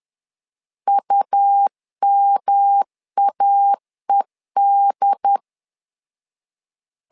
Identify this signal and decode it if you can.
Morse code, heard as a keyed tone, about 10 wpm, 800 Hz. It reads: UMAED
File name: LEZIONI / 3 MWRO / 3 M 10wpm 1828